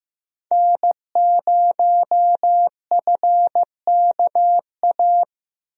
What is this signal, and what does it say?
Morse code, heard as a keyed tone, about 15 wpm, 700 Hz. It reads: N0FKA
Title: USA Call Signs - 5 Characters - Repeated 3x - 15wpm